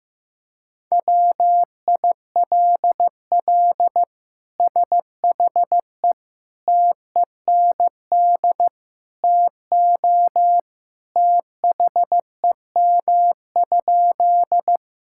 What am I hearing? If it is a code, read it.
Morse code, heard as a keyed tone, about 15 wpm, 700 Hz. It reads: WILL SHE TEND TO THEM?
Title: Sentences from Top 1000 Words - Review Part 5 - 15wpm